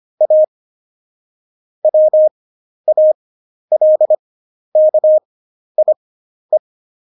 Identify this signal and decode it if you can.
Morse code, heard as a keyed tone, about 25 wpm, spaced as for 11 wpm, 620 Hz. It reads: A WALKIE
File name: Jokes-25-char-10-eff